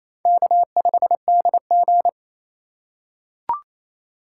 Morse code, heard as a keyed tone, about 28 wpm, 700 Hz. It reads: K5DG
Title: USA Call Signs - 4 Characters - 2x Character Spacing - 28wpm